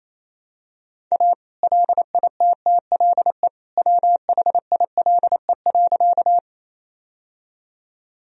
Morse code, heard as a keyed tone, about 28 wpm, 700 Hz. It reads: A LITTLE WHILE.
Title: Sentences from Top 1000 Words - Review Part 1 - 28wpm